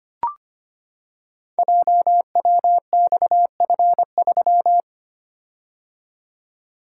Morse code, heard as a keyed tone, about 25 wpm, 700 Hz. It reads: JWXF3
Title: Code Groups - 5 Letter:Numbers - 25wpm